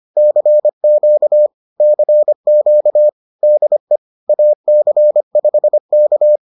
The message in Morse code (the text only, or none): CQ CQ DE AC5K